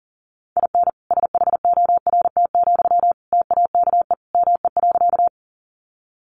Morse code, heard as a keyed tone, about 40 wpm, 700 Hz. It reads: IN SHORT, TAKE ME.
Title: Sentences from Top 400 Words 40wpm